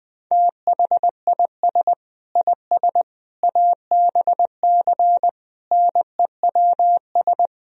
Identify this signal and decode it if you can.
Morse code, heard as a keyed tone, about 20 wpm, 700 Hz. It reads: THIS IS ABC NEWS
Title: Speed-Racing - Sets of 5 Words 17wpm